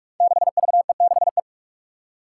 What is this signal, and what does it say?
Morse code, heard as a keyed tone, about 47 wpm, 700 Hz. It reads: 6VE6E
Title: Five Challenging Letter-Number Combinations 45wpm